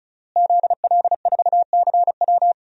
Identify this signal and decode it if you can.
Morse code, heard as a keyed tone, about 36 wpm, 700 Hz. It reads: ZL4CW
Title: DXCC-100 Call Signs - 5 Characters - Repeated 2x - 35wpm